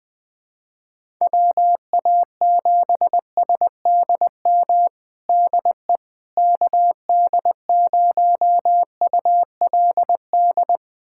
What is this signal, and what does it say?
Morse code, heard as a keyed tone, about 20 wpm, 700 Hz. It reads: WA7SDM DE KD0ULD
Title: USA Callsign DE Callsign - 6 Characters - 20wpm